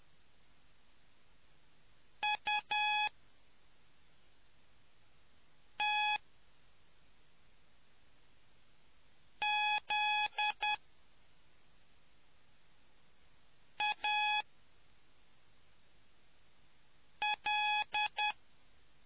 Morse code, heard as a keyed tone, about 10 wpm, spaced as for 3 wpm, 840 Hz. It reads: UTZAL